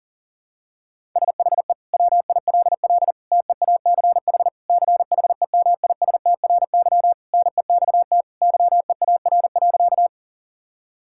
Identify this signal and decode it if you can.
Morse code, heard as a keyed tone, about 40 wpm, 700 Hz. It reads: SHE WILL TEACH CHEMISTRY NEXT YEAR.
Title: Sentences from Top 1000 Words - Review Part 6 - 40wpm